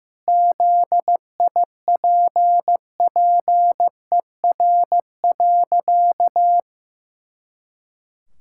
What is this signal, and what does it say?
Morse code, heard as a keyed tone, about 15 wpm, 700 Hz. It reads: ZIPPER.
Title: Pangram Challenge v2 - 15wpm